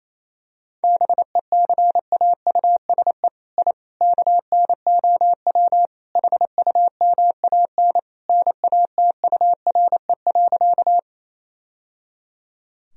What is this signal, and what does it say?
Morse code, heard as a keyed tone, about 27 wpm, 700 Hz. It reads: BECAUSE I KNOW HUMAN NATURE.
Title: Sentences from Top 1000 Words - Review Part 3 - 28wpm